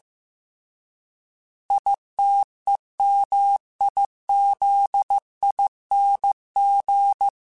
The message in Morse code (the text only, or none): ITEMIZING